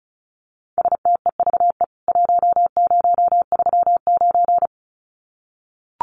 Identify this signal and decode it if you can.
Morse code, heard as a keyed tone, about 34 wpm, 700 Hz. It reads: STEVE 1039
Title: CWT Exchange 35wpm